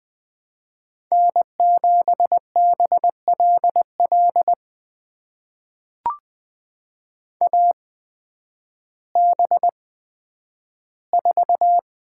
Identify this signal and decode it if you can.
Morse code, heard as a keyed tone, about 20 wpm, spaced as for 11 wpm, 700 Hz. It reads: N7BLL AB4
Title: USA Call Signs - 5 Characters - 8x Character Spacing - 20wpm